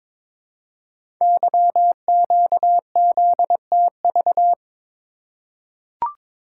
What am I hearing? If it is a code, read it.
Morse code, heard as a keyed tone, about 22 wpm, 700 Hz. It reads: YQZTV
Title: Code Groups - 5 Letter:Numbers - 22wpm